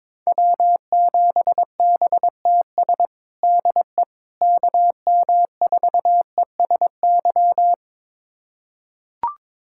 Morse code, heard as a keyed tone, about 22 wpm, 700 Hz. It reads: W7BTS DE KM4ESY
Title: USA Callsign DE Callsign 22wpm